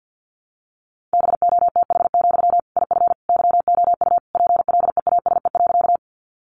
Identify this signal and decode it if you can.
Morse code, heard as a keyed tone, about 50 wpm, 700 Hz. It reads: BOTH, IF YOU PLEASE.
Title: Sentences from Top 1500 Words 50wpm